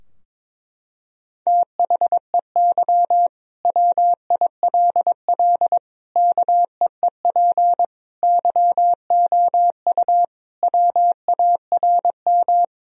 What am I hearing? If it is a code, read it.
Morse code, heard as a keyed tone, about 21 wpm, 700 Hz. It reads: THEY WILL KEEP YOU WARM
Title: Sentences from Top 1000 Words - Review Part 3 - 22wpm